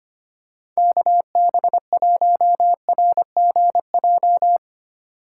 Morse code, heard as a keyed tone, about 25 wpm, 700 Hz. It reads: KB1RGJ
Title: USA Call Signs - 3x Character Spacing - 25wpm